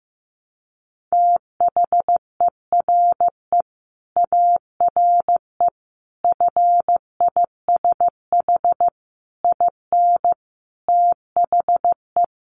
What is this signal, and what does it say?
Morse code, heard as a keyed tone, about 14 wpm, 700 Hz. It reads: THERE ARE FISH IN THE
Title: Sentences from Top 1100 Words 15wpm